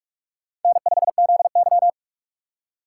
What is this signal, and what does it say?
Morse code, heard as a keyed tone, about 46 wpm, 700 Hz. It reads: N5ZY
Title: USA Call Signs - 4 Characters - 4x Character Spacing - 45wpm